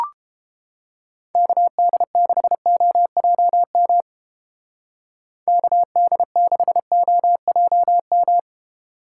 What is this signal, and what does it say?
Morse code, heard as a keyed tone, about 29 wpm, 700 Hz. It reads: KD6OJM KD6OJM